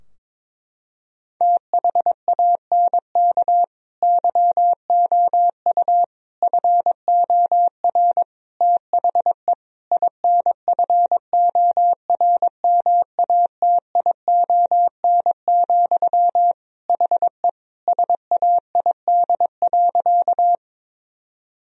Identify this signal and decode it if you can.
Morse code, heard as a keyed tone, about 22 wpm, 700 Hz. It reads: THANK YOU FOR THE INFORMATION, HE SAID.
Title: Sentences from Top 1000 Words - Review Part 3 - 22wpm